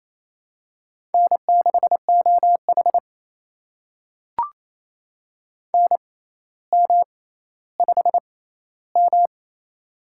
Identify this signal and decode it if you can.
Morse code, heard as a keyed tone, about 28 wpm, spaced as for 14 wpm, 700 Hz. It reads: N6OH NM5M
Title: USA Call Signs - 4 Characters - 6x Character Spacing - 28wpm